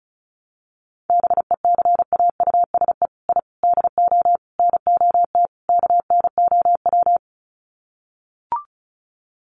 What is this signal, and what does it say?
Morse code, heard as a keyed tone, about 34 wpm, 700 Hz. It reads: BECAUSE I DO NOT KNOW
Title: Sets of 5 Words Encore2 35wpm